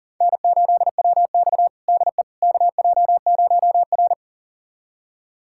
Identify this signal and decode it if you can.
Morse code, heard as a keyed tone, about 40 wpm, 700 Hz. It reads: N8WX DE KJ0R
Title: USA Callsign DE Callsign - 4 Characters - 40wpm